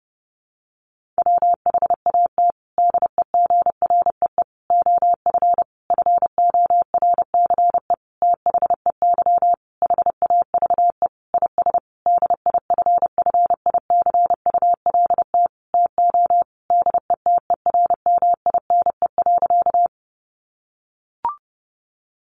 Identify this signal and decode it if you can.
Morse code, heard as a keyed tone, about 30 wpm, 700 Hz. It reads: WHAT DEGREE OF FORCE THEY HAVE IS DIFFICULT TO DETERMINE.
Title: Sentences from Top 1500 Words 30wpm